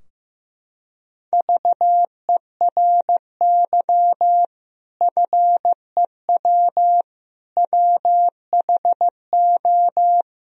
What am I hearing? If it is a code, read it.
Morse code, heard as a keyed tone, about 15 wpm, 700 Hz. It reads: VERY FEW WHO
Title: Sentences from Top 1000 Words - Review Part 2 - 15wpm